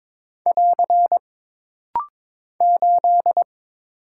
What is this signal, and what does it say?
Morse code, heard as a keyed tone, about 22 wpm, 700 Hz. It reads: <AR> 8
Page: This rapid-fire timing is geared to push your ICR (Instant Character Recognition) skills.